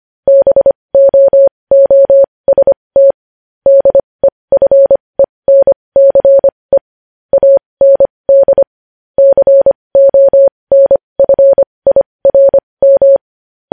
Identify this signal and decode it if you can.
Morse code, heard as a keyed tone, about 25 wpm, spaced as for 20 wpm, 550 Hz. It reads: BOOST DEFENCE AND CONFIRM
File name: podcast / News headlines-1x-25x20